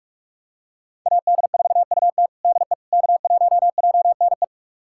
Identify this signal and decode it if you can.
Morse code, heard as a keyed tone, about 46 wpm, 700 Hz. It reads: AD4UT DE K1JNE
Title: USA Callsign DE Callsign - 5 Characters - 45wpm